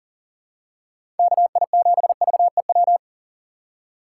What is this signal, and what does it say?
Morse code, heard as a keyed tone, about 40 wpm, 690 Hz. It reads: KI7VEW
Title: USA Call Signs - 6 Characters - 2x Character Spacing - 40wpm